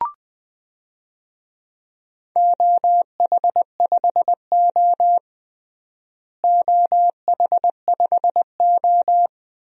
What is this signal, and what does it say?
Morse code, heard as a keyed tone, about 20 wpm, 700 Hz. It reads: OH5O OH5O
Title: DXCC-100 Call Signs - 4 Characters - Repeated 2x - 20wpm